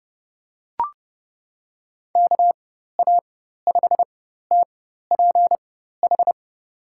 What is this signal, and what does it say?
Morse code, heard as a keyed tone, about 31 wpm, spaced as for 14 wpm, 700 Hz. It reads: KA5TPH
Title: USA Call Signs - 6 Characters - 4x Character Spacing - 30wpm